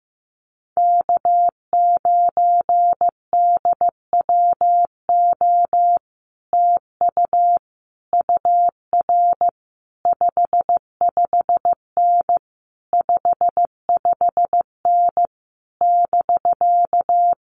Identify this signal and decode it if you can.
Morse code, heard as a keyed tone, about 15 wpm, 700 Hz. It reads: K9DWO TU UR 55N 55N <BK>
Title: Common QSO Phrases Encore 15wpm